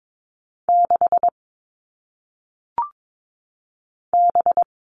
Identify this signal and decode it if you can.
Morse code, heard as a keyed tone, about 22 wpm, 700 Hz. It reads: 6 B
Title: All Cut Numbers - 1-Digit 22wpm